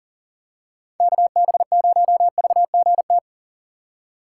Morse code, heard as a keyed tone, about 40 wpm, 700 Hz. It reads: KB0VGT
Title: USA Call Signs - 6 Characters - 6x Character Spacing - 40wpm